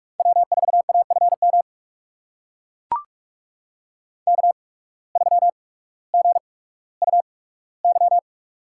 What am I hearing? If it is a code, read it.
Morse code, heard as a keyed tone, about 44 wpm, spaced as for 19 wpm, 700 Hz. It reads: W4AFM K3GUY
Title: USA Call Signs - 5 Characters - 8x Character Spacing - 45wpm